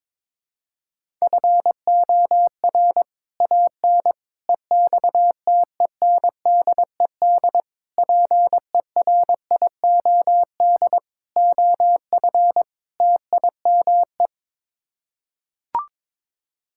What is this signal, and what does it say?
Morse code, heard as a keyed tone, about 22 wpm, 700 Hz. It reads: FOR AN EXTENDED PERIOD OF TIME
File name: practice / 22wpm / Sets of 6 Words - 22wpm